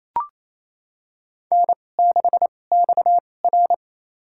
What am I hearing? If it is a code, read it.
Morse code, heard as a keyed tone, about 28 wpm, spaced as for 20 wpm, 700 Hz. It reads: N6XR
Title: USA Call Signs - 2x Character Spacing - 28wpm